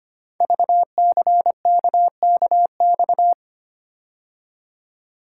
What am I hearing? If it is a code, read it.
Morse code, heard as a keyed tone, about 25 wpm, 700 Hz. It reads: VCKKX
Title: Code Groups - 5 Letters - 25wpm